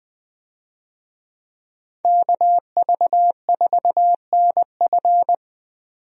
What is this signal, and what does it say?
Morse code, heard as a keyed tone, about 21 wpm, 700 Hz. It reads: KV4NF
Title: USA Call Signs - 5 Characters - 20wpm